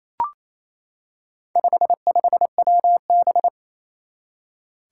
Morse code, heard as a keyed tone, about 28 wpm, 700 Hz. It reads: 55WB
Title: Four Challenging Letter-Number Combinations 28wpm